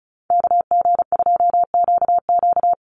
Morse code, heard as a keyed tone, about 34 wpm, 700 Hz. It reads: KG2QQ
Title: USA Call Signs 35wpm